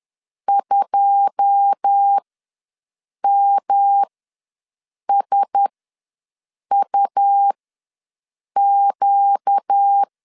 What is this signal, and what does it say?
Morse code, heard as a keyed tone, about 10 wpm, spaced as for 6 wpm, 800 Hz. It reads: 2MSUQ
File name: LEZIONI / 9 286 ws 10wpm 4032